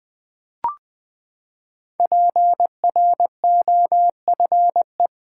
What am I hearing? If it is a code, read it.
Morse code, heard as a keyed tone, about 20 wpm, 700 Hz. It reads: PROFE
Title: Build A Word - 5 Letters At A Time - 12-Letter Words - 20wpm